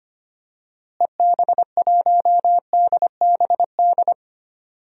Morse code, heard as a keyed tone, about 25 wpm, 700 Hz. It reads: EB1DBD
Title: DXCC-100 Call Signs - 6 Characters - Repeated 3x - 25wpm